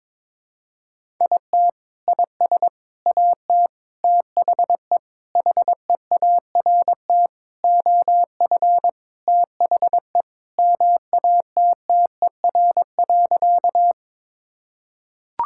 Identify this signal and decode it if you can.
Morse code, heard as a keyed tone, about 22 wpm, 700 Hz. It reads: IT IS AT THE HEART OF THE MATTER.
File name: practice / 22wpm / Sentences with 4-Word Phrases - 22wpm